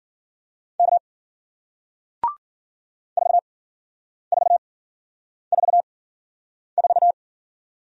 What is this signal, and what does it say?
Morse code, heard as a keyed tone, about 53 wpm, spaced as for 9 wpm, 700 Hz. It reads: K 4444